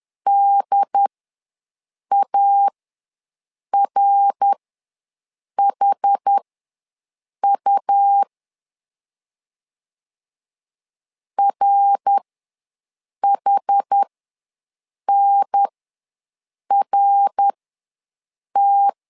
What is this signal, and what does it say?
Morse code, heard as a keyed tone, about 11 wpm, spaced as for 6 wpm, 800 Hz. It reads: DARHU RHNRT
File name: LEZIONI / 3 MWRO / 3 MWR ws 10wpm 2151